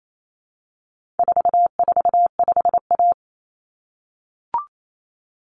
Morse code, heard as a keyed tone, about 28 wpm, 700 Hz. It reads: 445A
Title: Common Cut Numbers - 4-Digit 28wpm